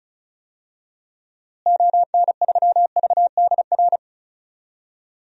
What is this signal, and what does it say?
Morse code, heard as a keyed tone, about 34 wpm, 700 Hz. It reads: ON3VDR
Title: DXCC-100 Call Signs - 6 Characters - Repeated 2x - 35wpm